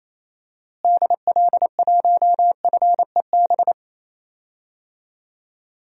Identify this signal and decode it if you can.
Morse code, heard as a keyed tone, about 27 wpm, 700 Hz. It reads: DL1FEB
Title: DXCC-100 Call Signs - 6 Characters - Repeated 3x - 28wpm